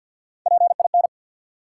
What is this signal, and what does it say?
Morse code, heard as a keyed tone, about 47 wpm, 700 Hz. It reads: PIN
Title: Three Letter Words - 50wpm